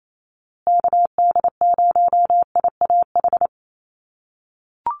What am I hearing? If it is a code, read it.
Morse code, heard as a keyed tone, about 28 wpm, 700 Hz. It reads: KD0IAH
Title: USA Call Signs Repeated 2x 28wpm